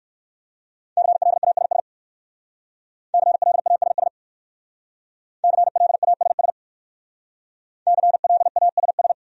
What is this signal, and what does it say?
Morse code, heard as a keyed tone, about 54 wpm, 700 Hz. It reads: CLASS CLASS CLASS CLASS
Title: Speed-Racing Top 500 Words 45wpm